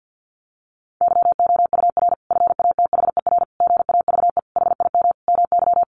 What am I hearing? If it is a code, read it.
Morse code, heard as a keyed tone, about 50 wpm, 700 Hz. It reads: YOUR FATHER GAVE HIM MY
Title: Sentences from Top 1400 Words 50wpm